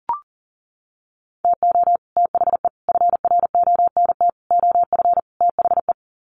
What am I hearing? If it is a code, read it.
Morse code, heard as a keyed tone, about 40 wpm, 700 Hz. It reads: TO THE FRONT OF THE
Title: Sets of 5 Words 40wpm